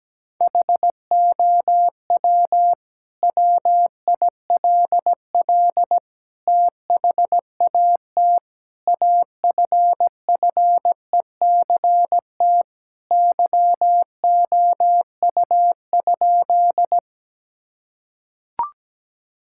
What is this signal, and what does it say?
Morse code, heard as a keyed tone, about 17 wpm, 700 Hz. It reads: HOW WILL THAT AFFECT YOU?
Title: Sentences from Top 1000 Words - Review Part 6 - 17wpm